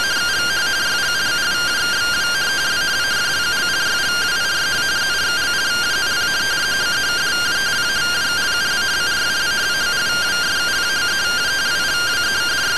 Two-tone teleprinter signal, about 170 Hz shift.
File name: findings / MR AXIAL 33,99mm 1.gen-02 rtty rx